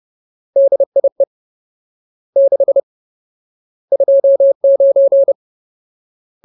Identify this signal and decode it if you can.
Morse code, heard as a keyed tone, about 30 wpm, 550 Hz. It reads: DIE 6 29
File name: Ultimate Word List / 30WPM Ultimate